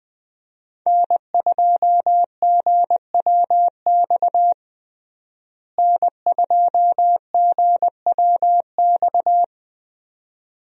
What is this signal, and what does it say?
Morse code, heard as a keyed tone, about 20 wpm, 700 Hz. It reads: N2GWX N2GWX
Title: USA Call Signs - 5 Characters - Repeated 2x - 20wpm